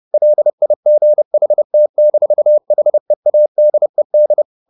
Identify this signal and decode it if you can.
Morse code, heard as a keyed tone, about 30 wpm, 600 Hz. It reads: LIGHT-HEADED